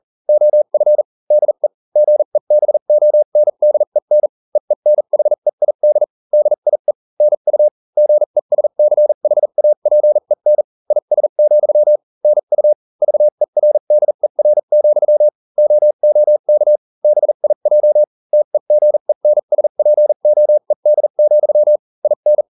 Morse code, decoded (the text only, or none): OF DE GEBONDEN EENHEID DIE NU GESCHAPEN IS, NU VERDER, OOK BIJ TEGENSPOED, IN